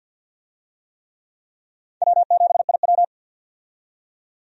Morse code, heard as a keyed tone, about 51 wpm, 700 Hz. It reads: W7IW
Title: USA Call Signs - 4 Characters - 50wpm